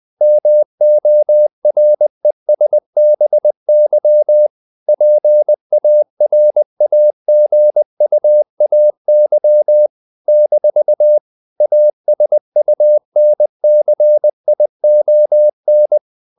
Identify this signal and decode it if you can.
Morse code, heard as a keyed tone, about 20 wpm, 600 Hz. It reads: MORESBY PARAGUAY - ASUNCION
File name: CW / Practice / NationalCapitals20